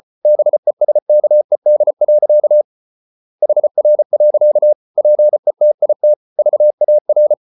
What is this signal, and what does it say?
Morse code, heard as a keyed tone, about 34 wpm, 600 Hz. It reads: BESKED. HR. PETIT VAR
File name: Ferie QRQ 34wpm